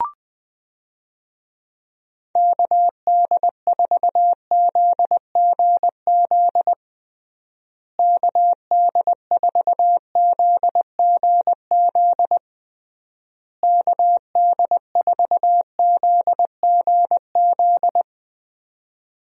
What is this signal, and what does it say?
Morse code, heard as a keyed tone, about 20 wpm, 700 Hz. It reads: KD4ZGZ KD4ZGZ KD4ZGZ
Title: USA Call Signs - 6 Characters - Repeated 3x - 20wpm